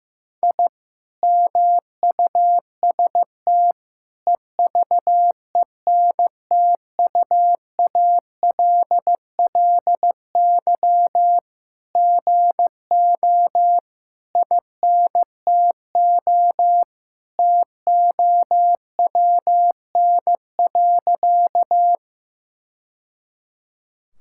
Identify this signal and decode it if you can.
Morse code, heard as a keyed tone, about 15 wpm, 700 Hz. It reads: I MUST EVENTUALLY GO INTO TOWN.
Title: Sentences from Top 1000 Words - Review Part 6 - 15wpm